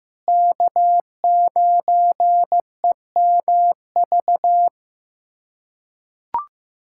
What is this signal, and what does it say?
Morse code, heard as a keyed tone, about 15 wpm, 700 Hz. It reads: K9EMV
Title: USA Callsign DE Callsign 15wpm